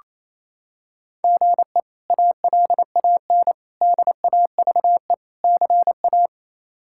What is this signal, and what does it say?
Morse code, heard as a keyed tone, about 27 wpm, 700 Hz. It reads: GE ALAN DAVE CA
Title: K1USN SST QSO Practice - 28wpm